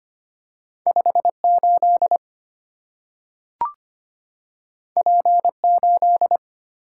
Morse code, heard as a keyed tone, about 25 wpm, 690 Hz. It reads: 58 P8
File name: Two Letter-Number Combinations - 25wpm